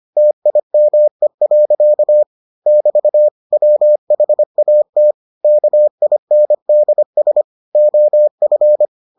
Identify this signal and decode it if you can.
Morse code, heard as a keyed tone, about 25 wpm, 600 Hz. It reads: TIME. <BT> WHAT KINDS OF